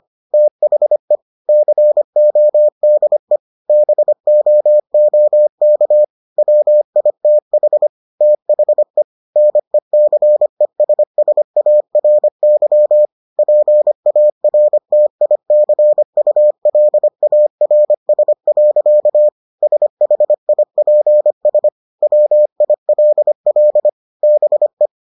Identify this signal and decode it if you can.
Morse code, heard as a keyed tone, about 25 wpm, 600 Hz. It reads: THE CODE BOOK WITH THE NECESSARY PARTICULARS. SHIPS WILL BE